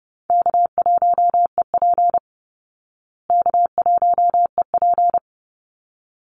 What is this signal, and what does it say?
Morse code, heard as a keyed tone, about 30 wpm, 700 Hz. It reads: K1EP K1EP
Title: Call Signs Repeated 3x - 30wpm